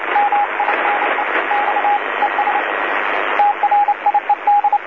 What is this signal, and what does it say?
Morse code, heard as a keyed tone, about 28 wpm, 820 Hz, with radio noise. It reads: MINE X ES TRIED
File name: DC-TRCVR / 80-CW